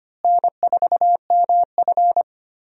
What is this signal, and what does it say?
Morse code, heard as a keyed tone, about 25 wpm, 700 Hz. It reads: N4MF
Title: USA Call Signs - 4 Characters - Repeated 3x - 25wpm